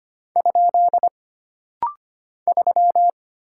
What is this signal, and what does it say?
Morse code, heard as a keyed tone, about 25 wpm, 700 Hz. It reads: ? 3
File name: Single Character (Letter:Number:Punctuation:Prosign) - Rapid-Fire - 25wpm